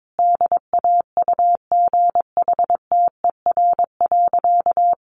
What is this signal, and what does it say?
Morse code, heard as a keyed tone, about 22 wpm, 700 Hz. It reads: DAUGHTER.
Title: Sentences from Top 1500 Words 22wpm